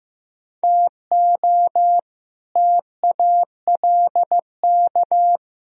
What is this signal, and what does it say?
Morse code, heard as a keyed tone, about 15 wpm, 700 Hz. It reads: TO TALK
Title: Speed-Racing - Sets of 4 Words 15wpm